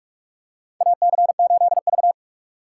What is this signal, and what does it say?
Morse code, heard as a keyed tone, about 46 wpm, 700 Hz. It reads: AC8V
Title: USA Call Signs - 3x Character Spacing - 45wpm